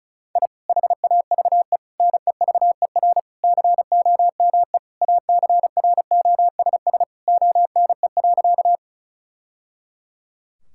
Morse code, heard as a keyed tone, about 34 wpm, 700 Hz. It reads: I HAVE NEVER COME ACROSS ONE.
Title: Sentences from Top 1000 Words - Review Part 5 - 35wpm